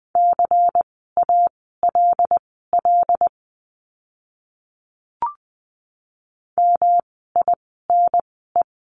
Morse code, heard as a keyed tone, about 20 wpm, spaced as for 14 wpm, 700 Hz. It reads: CALL MINE
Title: 108 Lesson 18f - Letter m - Words 20wpm